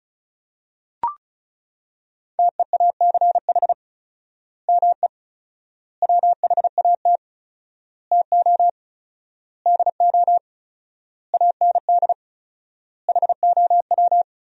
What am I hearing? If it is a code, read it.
Morse code, heard as a keyed tone, about 34 wpm, 700 Hz. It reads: TEACH ME WHAT TO DO AND HOW
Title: 4x Word-Spacing - Sentences from Top 500 Words - 35wpm